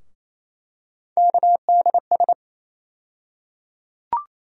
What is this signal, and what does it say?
Morse code, heard as a keyed tone, about 28 wpm, 700 Hz. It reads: KDS
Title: USA Call Sign Postfixes - 3 Letters 28wpm